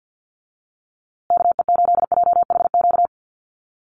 Morse code, heard as a keyed tone, about 50 wpm, 700 Hz. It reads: KE8JHQ
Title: USA Call Signs Repeated 2x 50wpm